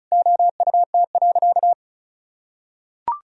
Morse code, heard as a keyed tone, about 34 wpm, 700 Hz. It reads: OUT.
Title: Sentences from Top 1000 Words - Review Part 4 - 35wpm